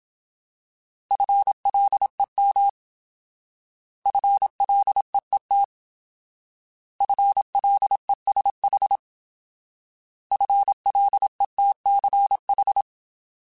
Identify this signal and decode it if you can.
Morse code, heard as a keyed tone, about 27 wpm, 800 Hz. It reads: FLEM FLEET FLESH FLETCH
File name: Alphabetical / Alphabetical mp3s / Fs @25